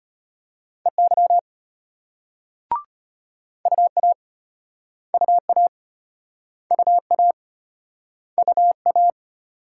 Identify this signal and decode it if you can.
Morse code, heard as a keyed tone, about 35 wpm, 700 Hz. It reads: EY UA UA UA UA